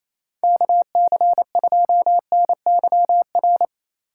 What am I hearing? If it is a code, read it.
Morse code, heard as a keyed tone, about 28 wpm, 700 Hz. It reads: KC2NYR
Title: USA Call Signs - 6 Characters - 4x Character Spacing - 28wpm